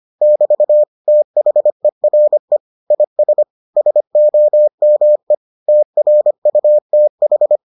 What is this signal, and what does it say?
Morse code, heard as a keyed tone, about 25 wpm, 600 Hz. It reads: <BT> THERE IS SOME TRUTH